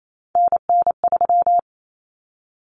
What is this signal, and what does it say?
Morse code, heard as a keyed tone, about 28 wpm, 700 Hz. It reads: NN3
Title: Common Cut Numbers - 3-Digit 28wpm